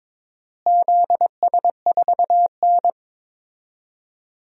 Morse code, heard as a keyed tone, about 22 wpm, 700 Hz. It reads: ZS4N
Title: DXCC-100 Call Signs - 4 Characters - Repeated 3x - 22wpm